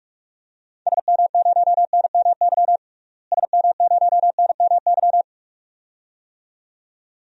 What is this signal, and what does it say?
Morse code, heard as a keyed tone, about 45 wpm, 700 Hz. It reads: SM0NMY SM0NMY
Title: DXCC-100 Call Signs - 6 Characters - Repeated 2x - 45wpm